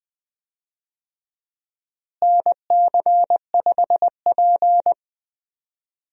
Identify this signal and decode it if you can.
Morse code, heard as a keyed tone, about 20 wpm, 700 Hz. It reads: NC5P
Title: USA Call Signs - 4 Characters - Repeated 2x - 20wpm